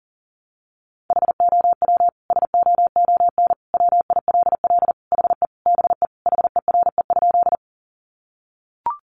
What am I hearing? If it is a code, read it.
Morse code, heard as a keyed tone, about 40 wpm, 700 Hz. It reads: HOW SOON WILL HE BE HERE?
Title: Sentences from Top 400 Words 40wpm